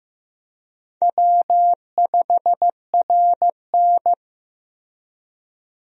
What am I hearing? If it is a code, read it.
Morse code, heard as a keyed tone, about 15 wpm, 700 Hz. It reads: W5RN
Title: USA Call Signs - 4 Characters - Repeated 2x - 15wpm